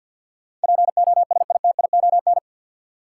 Morse code, heard as a keyed tone, about 53 wpm, 700 Hz. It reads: POSITION
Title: Sentences from Top 900 Words - New Word Drill - 50wpm